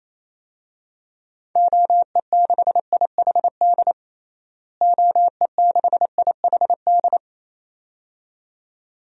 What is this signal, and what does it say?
Morse code, heard as a keyed tone, about 28 wpm, 700 Hz. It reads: OE6IHD OE6IHD
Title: DXCC-100 Call Signs - 6 Characters - Repeated 2x - 28wpm